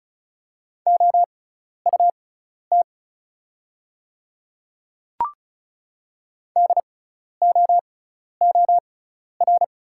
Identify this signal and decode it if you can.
Morse code, heard as a keyed tone, about 34 wpm, spaced as for 12 wpm, 690 Hz. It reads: OUT DOOR
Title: 6x Character Spacing - Top 500 Words - 35wpm